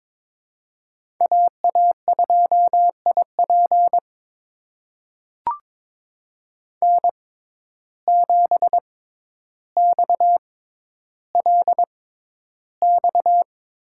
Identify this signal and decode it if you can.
Morse code, heard as a keyed tone, about 22 wpm, spaced as for 11 wpm, 700 Hz. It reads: AA2IP N7XLX
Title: USA Call Signs - 5 Characters - 6x Character Spacing - 22wpm